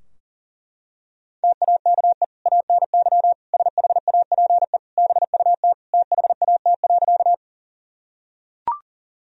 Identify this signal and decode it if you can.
Morse code, heard as a keyed tone, about 40 wpm, 700 Hz. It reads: TAKE ANY SHAPE BUT THAT.
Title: Sentences from Top 1000 Words - Review Part 3 - 40wpm